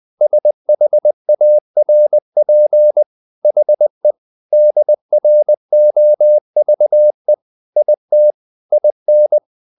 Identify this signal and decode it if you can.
Morse code, heard as a keyed tone, about 20 wpm, 600 Hz. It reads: SHARP HE DROVE IT IN
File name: the old man and the sea part 2 20wpm